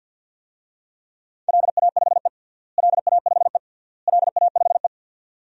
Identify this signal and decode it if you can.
Morse code, heard as a keyed tone, about 53 wpm, 700 Hz. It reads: LA5E LA5E LA5E
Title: DXCC-100 Call Signs - 4 Characters - Repeated 3x - 50wpm